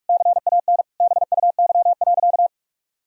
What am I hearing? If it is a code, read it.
Morse code, heard as a keyed tone, about 41 wpm, 700 Hz. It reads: CAN BUY.
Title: Sentences from Top 1000 Words - Review Part 4 - 45wpm